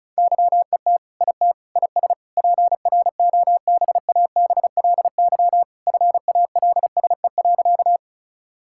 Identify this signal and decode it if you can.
Morse code, heard as a keyed tone, about 34 wpm, 700 Hz. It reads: YET IT IS PROBABLY FALSE.
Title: Sentences from Top 1600 Words 35wpm